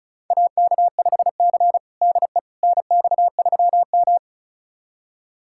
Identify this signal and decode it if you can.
Morse code, heard as a keyed tone, about 36 wpm, 700 Hz. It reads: AK5C DE NX3M
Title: USA Callsign DE Callsign - 4 Characters - 35wpm